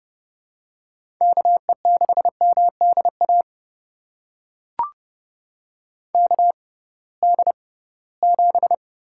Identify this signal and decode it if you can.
Morse code, heard as a keyed tone, about 30 wpm, spaced as for 19 wpm, 700 Hz. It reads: KE6MDA KD7
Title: USA Call Signs - 6 Characters - 6x Character Spacing - 30wpm